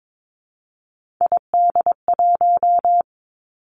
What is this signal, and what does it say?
Morse code, heard as a keyed tone, about 22 wpm, 700 Hz. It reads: ID1
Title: Three Challenging Letter-Number Combinations 22wpm